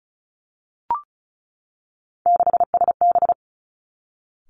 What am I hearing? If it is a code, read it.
Morse code, heard as a keyed tone, about 36 wpm, 700 Hz. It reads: 6SB
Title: Three Challenging Letter-Number Combinations 35wpm